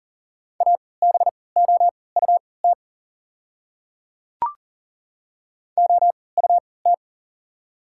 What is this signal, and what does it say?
Morse code, heard as a keyed tone, about 40 wpm, spaced as for 23 wpm, 690 Hz. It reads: ABOUT OUT
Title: 3x Character Spacing - Top 100 Words - 40wpm